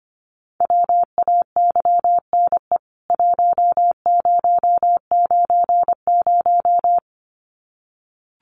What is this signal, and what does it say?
Morse code, heard as a keyed tone, about 25 wpm, 700 Hz. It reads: WAYNE 1090
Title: CWT Exchange 25wpm